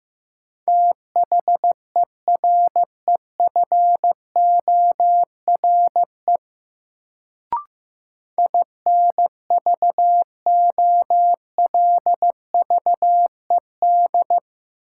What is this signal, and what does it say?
Morse code, heard as a keyed tone, about 15 wpm, 700 Hz. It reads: THEREFORE INVOLVED
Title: Sentences from Top 800 Words - New Word Drill - 15wpm